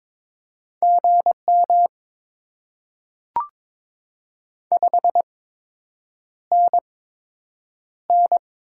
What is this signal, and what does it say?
Morse code, heard as a keyed tone, about 22 wpm, spaced as for 8 wpm, 700 Hz. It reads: GM 5NN
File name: Top 20 QSO Elements - 8x Character Spacing - 22wpm